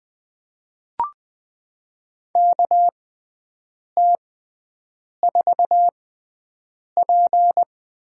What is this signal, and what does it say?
Morse code, heard as a keyed tone, about 20 wpm, spaced as for 7 wpm, 700 Hz. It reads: KT4P
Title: USA Call Signs - 5 Characters - 6x Character Spacing - 20wpm